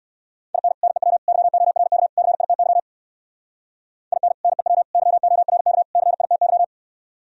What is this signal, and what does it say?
Morse code, heard as a keyed tone, about 58 wpm, 700 Hz. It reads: IN NEW YORK CITY IN NEW YORK CITY
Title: Speed-Racing - Sets of 4 Words 50wpm